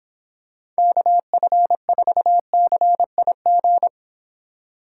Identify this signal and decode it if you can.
Morse code, heard as a keyed tone, about 26 wpm, 700 Hz. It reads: KF4CIG